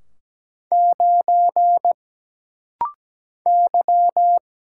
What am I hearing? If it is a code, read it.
Morse code, heard as a keyed tone, about 17 wpm, 700 Hz. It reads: 9 Y
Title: Single Letter-Number Alt - Rapid-Fire 17wpm